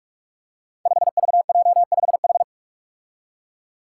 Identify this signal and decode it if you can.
Morse code, heard as a keyed tone, about 44 wpm, 700 Hz. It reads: 5VJ5H
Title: Five Challenging Letter-Number Combinations 45wpm